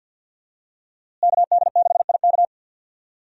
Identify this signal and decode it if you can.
Morse code, heard as a keyed tone, about 52 wpm, 700 Hz. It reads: KD6IK
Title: USA Call Signs - 5 Characters - Repeated 2x - 50wpm